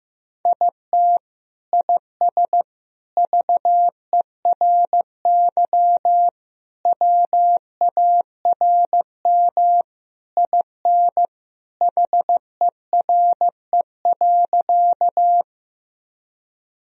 Morse code, heard as a keyed tone, about 15 wpm, 700 Hz. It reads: IT IS VERY WARM IN HERE.
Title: Sentences from Top 1000 Words - Review Part 2 - 15wpm